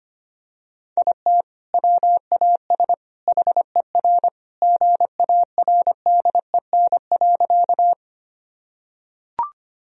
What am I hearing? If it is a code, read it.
Morse code, heard as a keyed tone, about 25 wpm, 700 Hz. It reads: IT WAS HER GARDEN.
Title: Sentences from Top 1000 Words - Review Part 4 - 25wpm